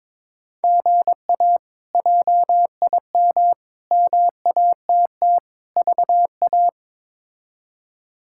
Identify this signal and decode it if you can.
Morse code, heard as a keyed tone, about 23 wpm, 700 Hz. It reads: GA JIM MATT VA
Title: K1USN SST QSO Practice - 22wpm